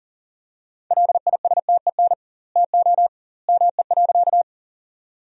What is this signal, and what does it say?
Morse code, heard as a keyed tone, about 40 wpm, 700 Hz. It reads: LISTEN TO ME.
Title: 2x Word-Spacing - Sentences from Top 400 Words - 40wpm